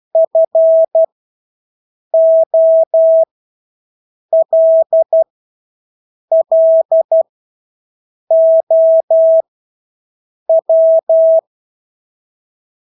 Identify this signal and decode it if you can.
Morse code, heard as a keyed tone, about 12 wpm, spaced as for 6 wpm, 650 Hz. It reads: FOLLOW
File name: Engelske ord 12-6 wpm